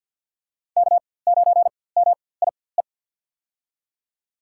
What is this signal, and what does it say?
Morse code, heard as a keyed tone, about 48 wpm, spaced as for 23 wpm, 700 Hz. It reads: K9MIE
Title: USA Call Signs - 5 Characters - 4x Character Spacing - 50wpm